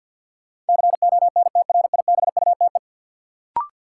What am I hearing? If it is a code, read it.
Morse code, heard as a keyed tone, about 52 wpm, 700 Hz. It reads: CONTRIBUTE
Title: Sentences from Top 1400 Words - New Word Drill Alt - 50wpm